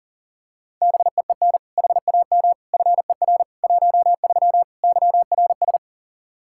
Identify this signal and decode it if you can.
Morse code, heard as a keyed tone, about 40 wpm, 700 Hz. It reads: BEEN HAM FER 13 YRS
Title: Practice QSO - Exchange 3 - More Info - 40wpm